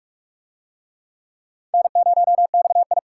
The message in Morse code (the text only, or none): N0XI